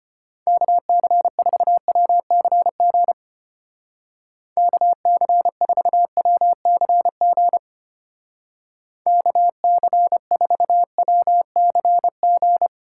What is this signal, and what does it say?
Morse code, heard as a keyed tone, about 31 wpm, 700 Hz. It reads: KC4WCG KC4WCG KC4WCG
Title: Speed Racing - USA 2x3 Call Signs 25wpm